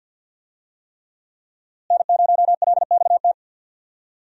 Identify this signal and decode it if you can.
Morse code, heard as a keyed tone, about 48 wpm, 700 Hz. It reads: N0LXT
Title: USA Call Signs - 5 Characters - Repeated 3x - 50wpm